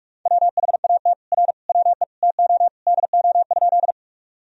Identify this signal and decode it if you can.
Morse code, heard as a keyed tone, about 46 wpm, 700 Hz. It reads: WHAT R WE TO DO?
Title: Sentences from Top 100 Words and Abbreviations 45wpm